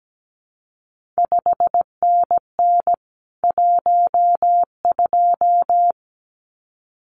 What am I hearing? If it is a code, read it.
Morse code, heard as a keyed tone, about 17 wpm, 700 Hz. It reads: 5NN 12
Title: CQ WW DX Contest Exchange Encore - 17wpm